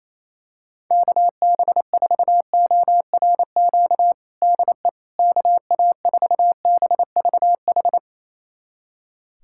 Morse code, heard as a keyed tone, about 28 wpm, 700 Hz. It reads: KB4ORQ DE KA4BVH
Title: USA Callsign DE Callsign - 6 Characters - 28wpm